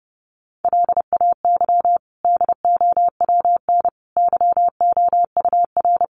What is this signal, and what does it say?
Morse code, heard as a keyed tone, about 30 wpm, 700 Hz. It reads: LAY DOWN YOUR
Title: Sentences from Top 1500 Words 30wpm